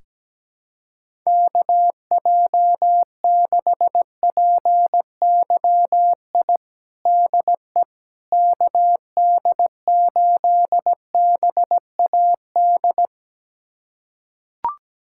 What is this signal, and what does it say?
Morse code, heard as a keyed tone, about 17 wpm, 700 Hz. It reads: KJ6PYI DE KD8BAD
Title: USA Callsign DE Callsign - 6 Characters - 17wpm